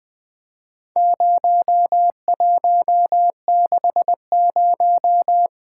Morse code, heard as a keyed tone, about 20 wpm, 700 Hz. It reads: 0160
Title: Four Letter-Number Combinations - 20wpm